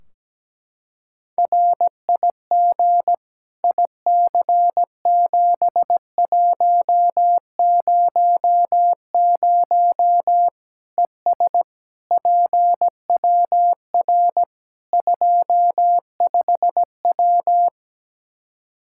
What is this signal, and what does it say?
Morse code, heard as a keyed tone, about 17 wpm, 700 Hz. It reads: RIG IC7100 ES PWR 25W
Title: K1USN SST QSO Practice 17wpm